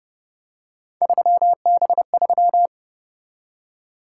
Morse code, heard as a keyed tone, about 29 wpm, 700 Hz. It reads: 3B3